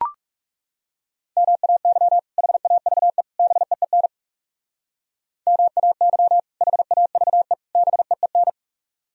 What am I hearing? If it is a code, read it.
Morse code, heard as a keyed tone, about 39 wpm, 700 Hz. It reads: MAY HAVE BEEN MAY HAVE BEEN